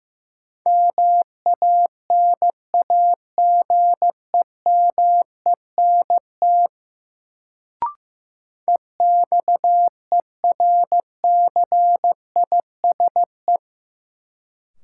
Morse code, heard as a keyed tone, about 16 wpm, 700 Hz. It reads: MANAGEMENT EXERCISE
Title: Sentences from Top 1000 Words - Review Part 4 - New Word Drill - 15wpm